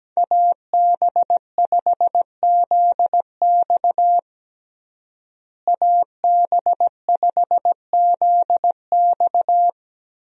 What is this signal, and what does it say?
Morse code, heard as a keyed tone, about 17 wpm, 700 Hz. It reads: AB5ZX AB5ZX
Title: USA Call Signs - 5 Characters - Repeated 3x - 17wpm